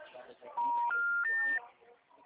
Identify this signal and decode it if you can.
Telephone call progress tone, heard as a telephone network's tone, with speech. Special information tone.